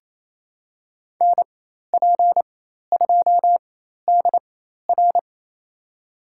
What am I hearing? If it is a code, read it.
Morse code, heard as a keyed tone, about 28 wpm, spaced as for 13 wpm, 700 Hz. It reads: NP2DR
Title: USA Call Signs - 5 Characters - 4x Character Spacing - 28wpm